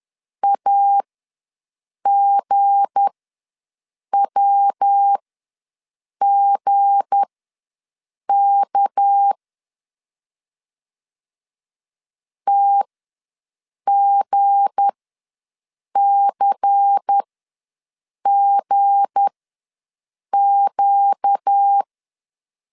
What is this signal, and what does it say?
Morse code, heard as a keyed tone, about 10 wpm, spaced as for 6 wpm, 800 Hz. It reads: AGWGK TGCGQ
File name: LEZIONI / 7 GF9Z / 7 G ws 10wpm 3843